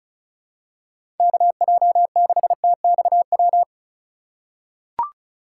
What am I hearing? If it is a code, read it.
Morse code, heard as a keyed tone, about 36 wpm, 700 Hz. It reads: KJ6TXW
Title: USA Call Signs - 6 Characters - 3x Character Spacing - 35wpm